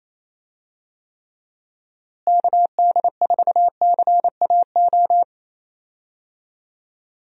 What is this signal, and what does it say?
Morse code, heard as a keyed tone, about 28 wpm, 700 Hz. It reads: KD4CAO
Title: USA Call Signs - 6 Characters - Repeated 2x - 28wpm